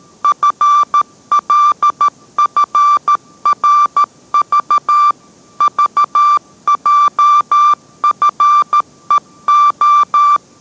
Morse code, heard as a keyed tone, about 13 wpm, 1180 Hz. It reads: FLFRV VJFEO